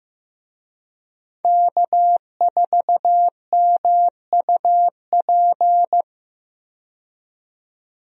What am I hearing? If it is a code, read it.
Morse code, heard as a keyed tone, about 15 wpm, 700 Hz. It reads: K4MUP
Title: USA Call Signs - 5 Characters - 15wpm